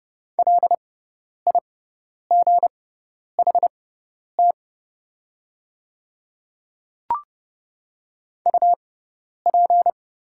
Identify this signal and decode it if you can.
Morse code, heard as a keyed tone, about 30 wpm, spaced as for 10 wpm, 690 Hz. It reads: LIGHT UP
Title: 6x Character Spacing - Top 200 Words - 30wpm